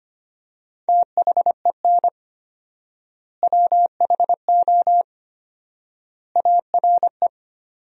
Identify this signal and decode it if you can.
Morse code, heard as a keyed tone, about 25 wpm, 700 Hz. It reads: THEN WHO ARE
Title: 4x Word-Spacing - Sentences from Top 100 Words - 25wpm